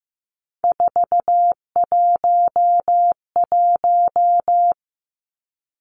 Morse code, heard as a keyed tone, about 15 wpm, 700 Hz. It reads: 411
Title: Three Challenging Letter-Number Combinations 15wpm